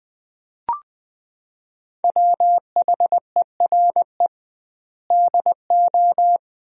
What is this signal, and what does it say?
Morse code, heard as a keyed tone, about 20 wpm, 700 Hz. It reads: WHERE DO
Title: 2x Word-Spacing - Sentences from Top 200 Words - 20wpm